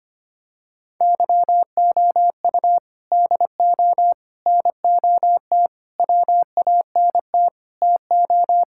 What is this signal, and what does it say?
Morse code, heard as a keyed tone, about 25 wpm, 700 Hz. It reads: YOU DO NOT WANT TO
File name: practice / 25wpm / Sentences with 4-Word Phrases - 25wpm